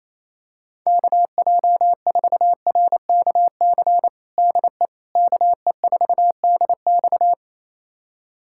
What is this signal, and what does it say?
Morse code, heard as a keyed tone, about 28 wpm, 700 Hz. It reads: KJ4RKC DE KE4DX
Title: USA Callsign DE Callsign 28wpm